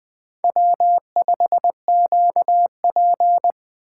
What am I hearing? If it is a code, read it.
Morse code, heard as a keyed tone, about 20 wpm, 700 Hz. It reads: W5QP
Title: K1USN SST QSO Practice - 20wpm